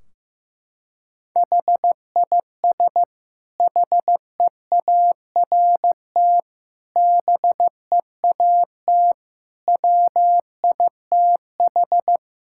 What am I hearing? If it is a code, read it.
Morse code, heard as a keyed tone, about 15 wpm, 700 Hz. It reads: HIS HEART BEAT WITH
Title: Sentences from Top 1600 Words 15wpm